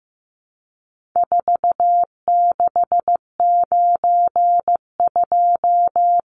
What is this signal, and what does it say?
Morse code, heard as a keyed tone, about 15 wpm, 700 Hz. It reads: 4692
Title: Numbers - 4-Digit 15wpm